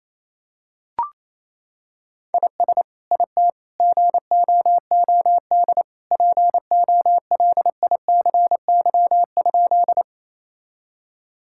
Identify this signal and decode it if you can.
Morse code, heard as a keyed tone, about 28 wpm, 700 Hz. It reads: IS IT GOOD POLICY?
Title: Sentences from Top 1000 Words - Review Part 5 - 28wpm